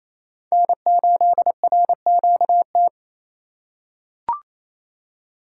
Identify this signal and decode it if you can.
Morse code, heard as a keyed tone, about 28 wpm, 700 Hz. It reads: N8RQT
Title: USA Call Signs - 5 Characters - 4x Character Spacing - 28wpm